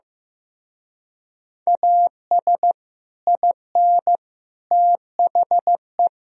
Morse code, heard as a keyed tone, about 15 wpm, 700 Hz. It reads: AS IN THE
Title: Speed-Racing - Sets of 5 Words 15wpm